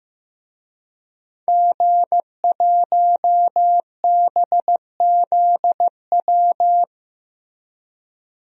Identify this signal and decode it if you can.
Morse code, heard as a keyed tone, about 15 wpm, 700 Hz. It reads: G1BZW
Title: DXCC-100 Call Signs - 5 Characters - Repeated 3x - 15wpm